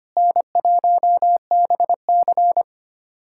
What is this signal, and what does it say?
Morse code, heard as a keyed tone, about 25 wpm, 700 Hz. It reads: N1BC
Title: USA Callsign DE Callsign - 4 Characters - 25wpm